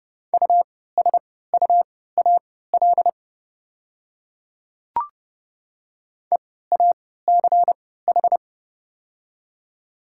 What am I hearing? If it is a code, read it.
Morse code, heard as a keyed tone, about 30 wpm, spaced as for 17 wpm, 690 Hz. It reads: USUAL EACH
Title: 3x Character Spacing - Top 500 Words - 30wpm